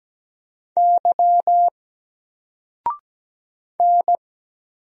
Morse code, heard as a keyed tone, about 17 wpm, 700 Hz. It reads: Y N
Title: Single Letter-Number 17wpm